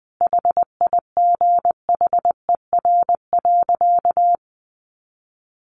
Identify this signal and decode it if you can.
Morse code, heard as a keyed tone, about 19 wpm, 700 Hz. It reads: HIGHER.
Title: Additional Easy Phrases 20wpm